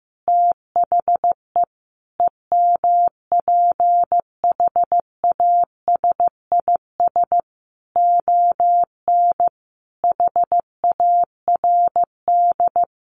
Morse code, heard as a keyed tone, about 15 wpm, 700 Hz. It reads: THE EMPHASIS ON HARD
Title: Sentences from Top 1400 Words 15wpm